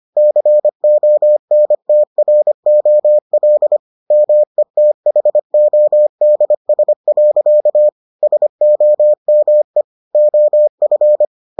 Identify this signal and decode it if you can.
Morse code, heard as a keyed tone, about 25 wpm, 600 Hz. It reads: CONTROL METHODS. SOME OF